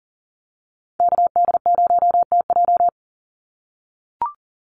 Morse code, heard as a keyed tone, about 39 wpm, 700 Hz. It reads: KD0TJ
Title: USA Call Signs 40wpm